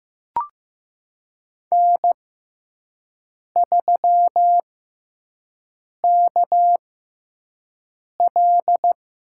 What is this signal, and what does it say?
Morse code, heard as a keyed tone, about 15 wpm, spaced as for 5 wpm, 700 Hz. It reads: N3KL
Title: USA Call Signs - 6x Character Spacing - 15wpm